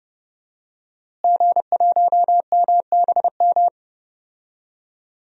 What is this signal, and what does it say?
Morse code, heard as a keyed tone, about 30 wpm, 700 Hz. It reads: G1MBM
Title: DXCC-100 Call Signs - 5 Characters - Repeated 2x - 30wpm